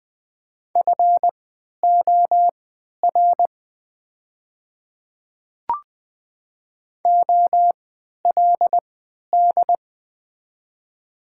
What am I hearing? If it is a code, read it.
Morse code, heard as a keyed tone, about 20 wpm, spaced as for 11 wpm, 700 Hz. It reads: FOR OLD
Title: 3x Character Spacing - Top 200 Words - 20wpm